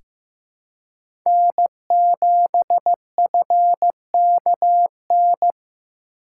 Morse code, heard as a keyed tone, about 15 wpm, 700 Hz. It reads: N7FKN
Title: USA Call Signs - 5 Characters - 2x Character Spacing - 15wpm